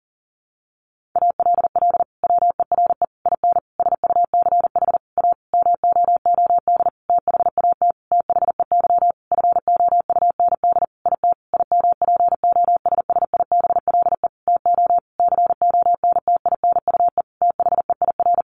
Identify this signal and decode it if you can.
Morse code, heard as a keyed tone, about 40 wpm, 700 Hz. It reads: ALL WERE IN SUCH A MOOD THAT THEY FOUND IT IMPOSSIBLE TO CONTINUE THEIR
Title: Sentences from Top 1100 Words 40wpm